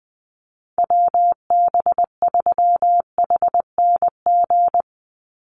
Morse code, heard as a keyed tone, about 20 wpm, 700 Hz. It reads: WB3HNG
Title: USA Call Signs 20wpm